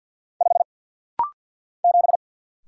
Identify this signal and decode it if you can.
Morse code, heard as a keyed tone, about 50 wpm, 700 Hz. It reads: 5 7
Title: Single Letter-Number - Rapid-Fire 50wpm